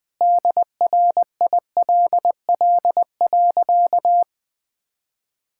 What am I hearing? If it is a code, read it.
Morse code, heard as a keyed tone, about 20 wpm, 700 Hz. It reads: DRILL.
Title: Sentences with 3-Word Phrases - 20wpm